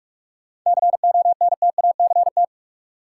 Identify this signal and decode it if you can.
Morse code, heard as a keyed tone, about 42 wpm, 700 Hz. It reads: CONTACT
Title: Sentences from Top 1000 Words - Review Part 7 - New Word Drill - 45wpm